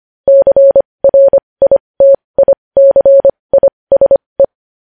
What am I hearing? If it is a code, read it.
Morse code, heard as a keyed tone, about 25 wpm, spaced as for 20 wpm, 550 Hz. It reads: CRITICISE
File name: podcast / News headlines-1x-25x20